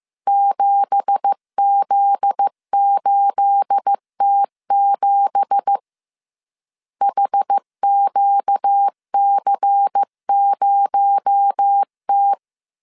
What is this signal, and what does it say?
Morse code, heard as a keyed tone, about 15 wpm, 800 Hz. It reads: 7Z8T7 HQC0T
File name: CW / DET / 1 9 1500C S A 14wpm 3222